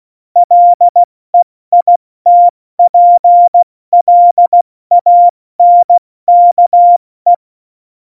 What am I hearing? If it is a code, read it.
Morse code, heard as a keyed tone, about 16 wpm, 700 Hz. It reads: LEITPLANKE
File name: Morsekurs / Witze / Witz 04 August 2020, 70 F80